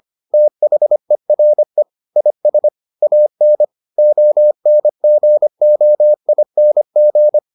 Morse code, heard as a keyed tone, about 25 wpm, 600 Hz. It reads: THERE IS AN ONGOING